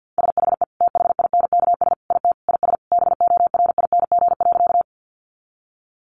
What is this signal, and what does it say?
Morse code, heard as a keyed tone, about 50 wpm, 700 Hz. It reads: SHE THINKS IT IS BORING.
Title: Easy Phrases 50wpm